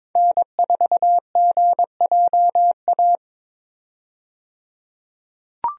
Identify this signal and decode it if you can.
Morse code, heard as a keyed tone, about 22 wpm, 700 Hz. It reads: N4GJA
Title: USA Call Signs - 5 Characters - 22wpm